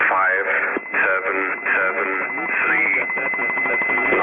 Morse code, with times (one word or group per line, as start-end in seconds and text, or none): none